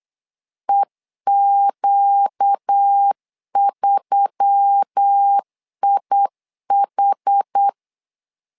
Morse code, heard as a keyed tone, about 8 wpm, 800 Hz. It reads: EQ3IH
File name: LEZIONI / 5 LBVC / 5 L 8wpm 2831